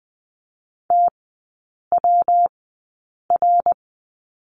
Morse code, heard as a keyed tone, about 20 wpm, spaced as for 8 wpm, 700 Hz. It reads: TWR
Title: 114 Lesson 19e - Letter w - Three Characters 20wpm